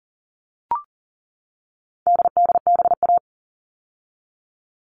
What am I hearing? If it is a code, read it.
Morse code, heard as a keyed tone, about 40 wpm, 700 Hz. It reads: DDBA
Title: All Cut Numbers - 4-Digit 40wpm